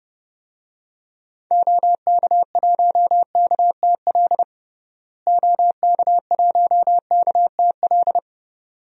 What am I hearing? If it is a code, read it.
Morse code, heard as a keyed tone, about 30 wpm, 700 Hz. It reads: OK1KTL OK1KTL
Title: DXCC-100 Call Signs - 6 Characters - Repeated 2x - 30wpm